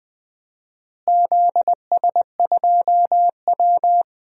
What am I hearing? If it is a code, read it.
Morse code, heard as a keyed tone, about 20 wpm, 700 Hz. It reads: ZS2W
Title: DXCC-100 Call Signs - 4 Characters - Repeated 2x - 20wpm